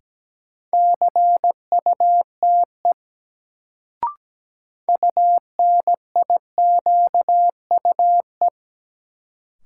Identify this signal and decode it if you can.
Morse code, heard as a keyed tone, about 17 wpm, 700 Hz. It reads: CUTE UNIQUE
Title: Sentences from Top 1000 Words - Review Part 7 - New Word Drill - 17wpm